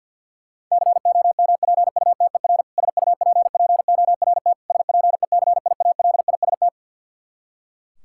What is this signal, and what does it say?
Morse code, heard as a keyed tone, about 51 wpm, 700 Hz. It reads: COMPUTER SUPPORT SPECIALIST
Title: Practice QSO - Exchange 3 - More Info - 50wpm